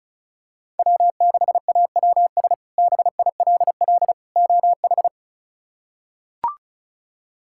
Morse code, heard as a keyed tone, about 36 wpm, 700 Hz. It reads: W6AWS BILL OH
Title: K1USN SST QSO Practice - 35wpm